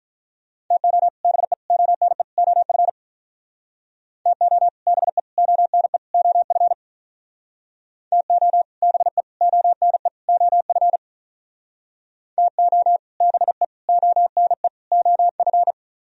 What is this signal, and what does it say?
Morse code, heard as a keyed tone, about 46 wpm, 700 Hz. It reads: TO BE ONE OF TO BE ONE OF TO BE ONE OF TO BE ONE OF